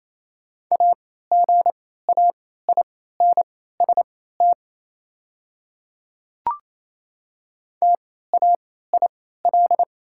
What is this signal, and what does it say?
Morse code, heard as a keyed tone, about 28 wpm, spaced as for 16 wpm, 690 Hz. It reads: AGAINST TAIL
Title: 3x Character Spacing - Top 500 Words - 28wpm